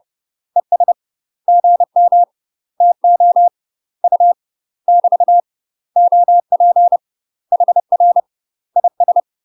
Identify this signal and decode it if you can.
Morse code, heard as a keyed tone, about 30 wpm, 700 Hz. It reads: ES GM TO U <BT> OP HR IS
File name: UK MASTER QSOs TRAINING / 30WPM / UK MASTER QSOs 0039